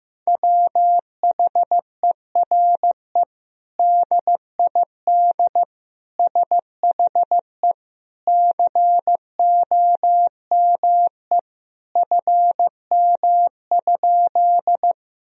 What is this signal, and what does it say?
Morse code, heard as a keyed tone, about 15 wpm, 700 Hz. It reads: WHERE DID SHE COME FM?
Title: Sentences from Top 200 Words and Abbreviations 15wpm